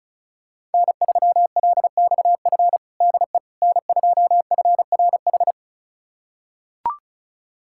Morse code, heard as a keyed tone, about 36 wpm, 700 Hz. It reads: N3LXF DE N2FRH
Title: USA Callsign DE Callsign - 5 Characters - 35wpm